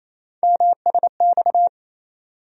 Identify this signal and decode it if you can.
Morse code, heard as a keyed tone, about 28 wpm, 700 Hz. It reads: MSX
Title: USA Call Sign Postfixes - 3 Letters 28wpm